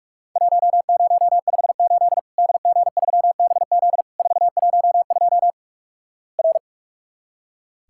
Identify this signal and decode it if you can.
Morse code, heard as a keyed tone, about 44 wpm, 700 Hz. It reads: 1058 DG3BZ 412 R
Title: Worked All Europe DX Contest - QTC Exchange - 45wpm